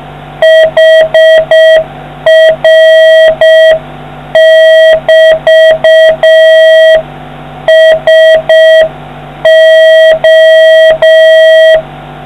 Morse code, heard as a keyed tone, about 6 wpm, 640 Hz, with noise. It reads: HR<BT>SO